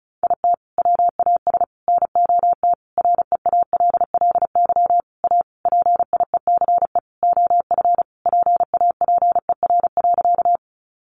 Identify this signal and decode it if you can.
Morse code, heard as a keyed tone, about 34 wpm, 700 Hz. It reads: IT WAS NOT REALLY A PIECE OF PAPER.
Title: Sentences from Top 600 Words 35wpm